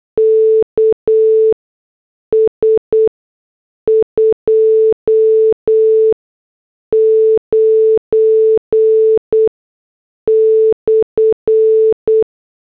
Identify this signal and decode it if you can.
Morse code, heard as a keyed tone, about 8 wpm, spaced as for 6 wpm, 440 Hz. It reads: KS 29/